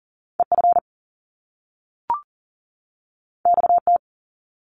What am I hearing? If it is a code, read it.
Morse code, heard as a keyed tone, about 41 wpm, 700 Hz. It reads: EF XT
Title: Top Two Letter Combinations in Words 40wpm